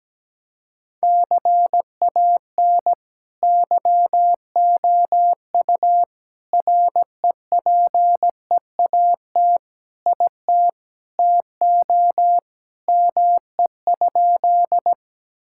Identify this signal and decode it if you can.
Morse code, heard as a keyed tone, about 17 wpm, 700 Hz. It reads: CAN YOU REPEAT IT TO ME?
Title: Sentences from Top 1600 Words 17wpm